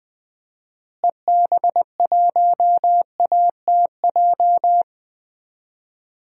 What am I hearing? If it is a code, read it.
Morse code, heard as a keyed tone, about 20 wpm, 700 Hz. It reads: EB1ATJ
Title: DXCC-100 Call Signs - 6 Characters - Repeated 2x - 20wpm